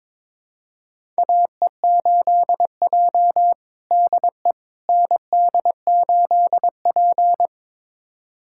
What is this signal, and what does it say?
Morse code, heard as a keyed tone, about 22 wpm, 700 Hz. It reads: AE8J DE ND8P
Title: USA Callsign DE Callsign - 4 Characters - 22wpm